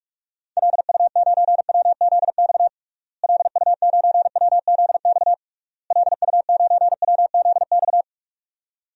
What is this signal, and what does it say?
Morse code, heard as a keyed tone, about 46 wpm, 700 Hz. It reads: LU9WZX LU9WZX LU9WZX
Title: DXCC-100 Call Signs - 6 Characters - Repeated 3x - 45wpm